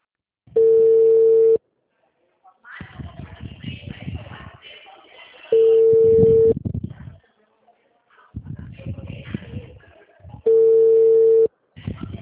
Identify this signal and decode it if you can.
Ringback tone (China, 450 Hz).